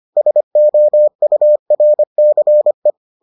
Morse code, heard as a keyed tone, about 25 wpm, 600 Hz. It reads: SOURCE